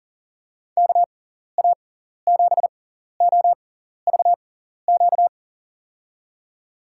Morse code, heard as a keyed tone, about 38 wpm, spaced as for 14 wpm, 700 Hz. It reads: KA7OVQ
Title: USA Call Signs - 6 Characters - 6x Character Spacing - 40wpm